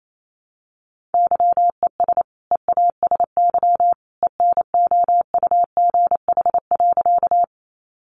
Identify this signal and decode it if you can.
Morse code, heard as a keyed tone, about 28 wpm, 700 Hz. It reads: YES EASY ENOUGH.
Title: Sentences from Top 700 Words 28wpm